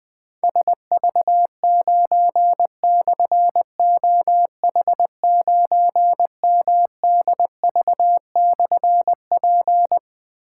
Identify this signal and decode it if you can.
Morse code, heard as a keyed tone, about 20 wpm, 700 Hz. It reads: SV9/OH9MDV/P
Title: Difficult Call Signs 20wpm